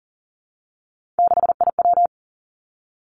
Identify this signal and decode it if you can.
Morse code, heard as a keyed tone, about 40 wpm, 700 Hz. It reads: 6IW
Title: Three Challenging Letter-Number Combinations 40wpm